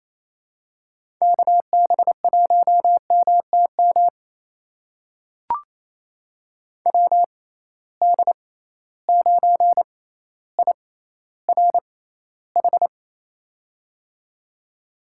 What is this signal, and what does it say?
Morse code, heard as a keyed tone, about 27 wpm, spaced as for 14 wpm, 700 Hz. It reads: KB1MTM WD9IRH
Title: USA Call Signs - 6 Characters - 6x Character Spacing - 28wpm